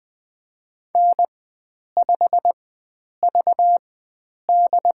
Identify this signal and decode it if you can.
Morse code, heard as a keyed tone, about 20 wpm, spaced as for 9 wpm, 700 Hz. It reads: N5VD
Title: USA Call Signs - 4x Character Spacing - 20wpm